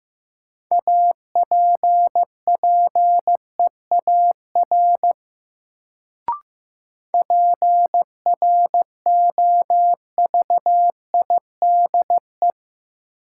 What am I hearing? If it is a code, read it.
Morse code, heard as a keyed tone, about 15 wpm, 700 Hz. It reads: APPEAR PROVIDE
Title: Sentences from Top 1000 Words - Review Part 2 - New Word Drill - 15wpm